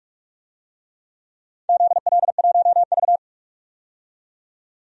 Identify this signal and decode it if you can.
Morse code, heard as a keyed tone, about 47 wpm, 700 Hz. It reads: ZL1V
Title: DXCC-100 Call Signs - 4 Characters - 45wpm